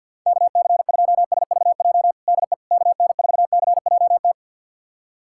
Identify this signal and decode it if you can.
Morse code, heard as a keyed tone, about 47 wpm, 700 Hz. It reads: KC2SVJ DE KN4CJT
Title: USA Callsign DE Callsign - 6 Characters - 50wpm